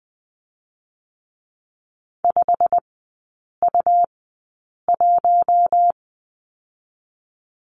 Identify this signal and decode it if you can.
Morse code, heard as a keyed tone, about 20 wpm, spaced as for 8 wpm, 690 Hz. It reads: 5U1
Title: 205 Lesson 33e - Letter j - Three Characters 20wpm